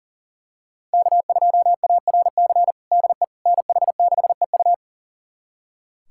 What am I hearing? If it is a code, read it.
Morse code, heard as a keyed tone, about 40 wpm, 700 Hz. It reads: K2ARC DE NH6EU
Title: USA Callsign DE Callsign - 5 Characters - 40wpm